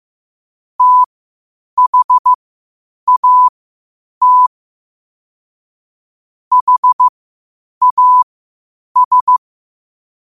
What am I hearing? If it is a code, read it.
Morse code, heard as a keyed tone, about 15 wpm, spaced as for 8 wpm, 1000 Hz. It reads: THAT HAS
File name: qotdfiles / 8wpm